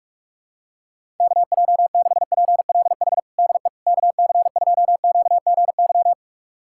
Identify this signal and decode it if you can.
Morse code, heard as a keyed tone, about 46 wpm, 700 Hz. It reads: KJ6PLH DE KC2QGY
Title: USA Callsign DE Callsign - 6 Characters - 45wpm